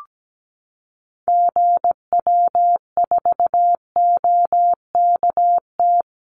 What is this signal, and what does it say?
Morse code, heard as a keyed tone, about 17 wpm, 700 Hz. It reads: GW4OKT
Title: Call Signs Repeated 2x - 17wpm